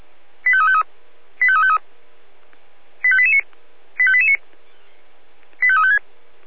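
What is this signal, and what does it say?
Five-tone selective call (ZVEI), heard as quick runs of five tones, tones of about 70 ms. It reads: 85353 85353 85889 85889 85356
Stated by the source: ZVEI five-tone selective call 25431 received by SDR.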